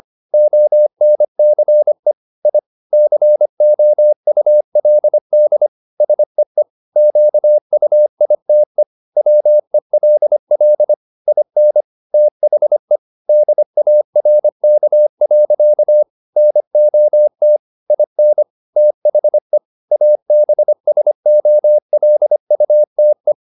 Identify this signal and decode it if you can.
Morse code, heard as a keyed tone, about 25 wpm, 600 Hz. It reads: ONCE I COULD SEE QUITE WELL IN THE DARK. NOT IN THE ABSOLUTE
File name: the old man and the sea part 1 25wpm